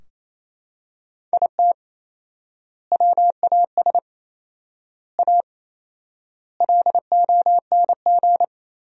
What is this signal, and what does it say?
Morse code, heard as a keyed tone, about 28 wpm, 700 Hz. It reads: IT WAS A LONG
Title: 4x Word-Spacing - Sentences from Top 500 Words - 28wpm